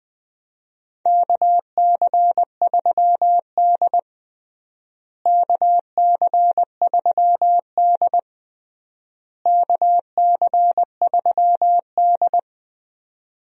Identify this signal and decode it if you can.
Morse code, heard as a keyed tone, about 20 wpm, 700 Hz. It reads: KC3D KC3D KC3D
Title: USA Call Signs - 4 Characters - Repeated 3x - 20wpm